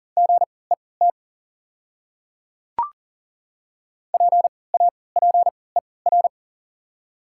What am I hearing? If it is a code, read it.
Morse code, heard as a keyed tone, about 40 wpm, spaced as for 23 wpm, 700 Hz. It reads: GET PAPER
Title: 3x Character Spacing - Top 300 Words - 40wpm